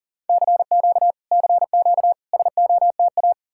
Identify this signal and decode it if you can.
Morse code, heard as a keyed tone, about 52 wpm, 700 Hz. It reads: CQ CQ SOTA